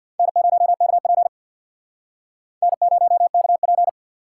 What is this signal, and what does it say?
Morse code, heard as a keyed tone, about 59 wpm, 700 Hz. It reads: N0KP N0KP
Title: Speed Racing - USA 1x2 Call Signs 50wpm